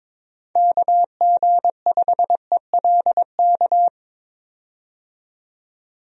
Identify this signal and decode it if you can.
Morse code, heard as a keyed tone, about 21 wpm, 700 Hz. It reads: KG5ELK
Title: USA Call Signs - 6 Characters - 22wpm